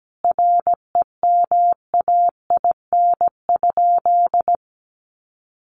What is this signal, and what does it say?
Morse code, heard as a keyed tone, about 17 wpm, 700 Hz. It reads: REMAIN?
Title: Sentences from Top 1200 Words 17wpm